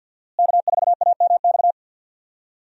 Morse code, heard as a keyed tone, about 51 wpm, 700 Hz. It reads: K4AMX
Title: USA Call Signs - 5 Characters - Repeated 2x - 50wpm